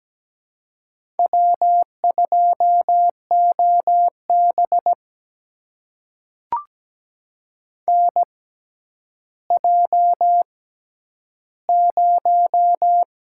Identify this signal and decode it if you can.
Morse code, heard as a keyed tone, about 17 wpm, spaced as for 10 wpm, 700 Hz. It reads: W2OB NJ0
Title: USA Call Signs - 6x Character Spacing - 17wpm